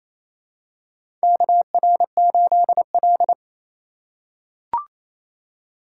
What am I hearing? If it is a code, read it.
Morse code, heard as a keyed tone, about 28 wpm, 700 Hz. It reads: KR8L
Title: K1USN SST QSO Practice - 28wpm